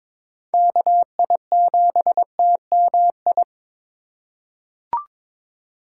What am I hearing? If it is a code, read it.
Morse code, heard as a keyed tone, about 22 wpm, 700 Hz. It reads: KI7TMI
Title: USA Call Signs - 4x Character Spacing - 22wpm